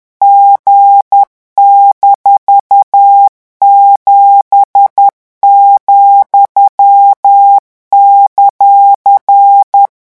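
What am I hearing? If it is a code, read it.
Morse code, heard as a keyed tone, about 10 wpm, 800 Hz. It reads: G-7,;
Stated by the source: French Morse code revision lesson 18 at 10 words per minute.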